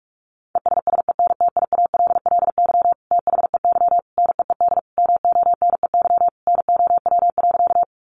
Morse code, heard as a keyed tone, about 44 wpm, 700 Hz. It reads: ESSENTIALLY THEY NEED MONEY NOW.
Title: Sentences from Top 1300 Words 45wpm